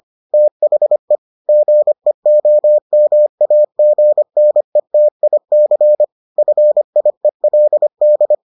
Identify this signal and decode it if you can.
Morse code, heard as a keyed tone, about 25 wpm, 600 Hz. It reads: THE GEOMAGNETIC FIELD